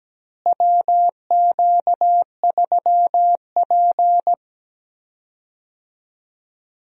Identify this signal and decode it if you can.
Morse code, heard as a keyed tone, about 17 wpm, 700 Hz. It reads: WQ3P
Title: USA Call Signs - 4 Characters - Repeated 3x - 17wpm